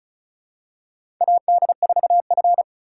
Morse code, heard as a keyed tone, about 36 wpm, 700 Hz. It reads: AD4F
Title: USA Call Signs - 4x Character Spacing - 35wpm